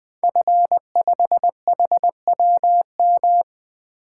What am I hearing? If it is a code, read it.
Morse code, heard as a keyed tone, about 20 wpm, 700 Hz. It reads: F5HWM
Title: Code Groups - 5 Letter:Numbers - 20wpm